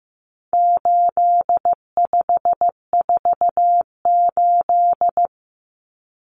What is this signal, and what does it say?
Morse code, heard as a keyed tone, about 15 wpm, 700 Hz. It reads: 8548
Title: Numbers - 4-Digit 15wpm